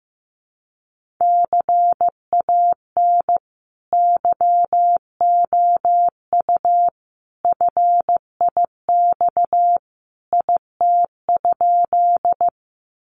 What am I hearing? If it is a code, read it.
Morse code, heard as a keyed tone, about 15 wpm, 700 Hz. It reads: CAN YOU FIX IT?
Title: Sentences from Top 1400 Words 15wpm